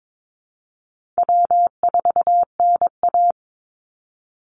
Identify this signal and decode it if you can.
Morse code, heard as a keyed tone, about 22 wpm, 700 Hz. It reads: W4NA
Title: Easy Call Signs - 22wpm